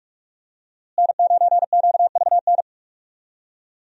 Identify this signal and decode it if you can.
Morse code, heard as a keyed tone, about 44 wpm, 700 Hz. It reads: N9QVN
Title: USA Call Signs - 5 Characters - Repeated 2x - 45wpm